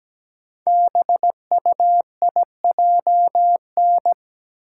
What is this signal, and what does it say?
Morse code, heard as a keyed tone, about 17 wpm, 700 Hz. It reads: BUIJN
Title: Code Groups - 5 Letters - 17wpm